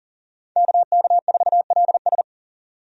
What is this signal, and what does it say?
Morse code, heard as a keyed tone, about 40 wpm, 700 Hz. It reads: KK4LS
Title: USA Call Signs - 5 Characters - 40wpm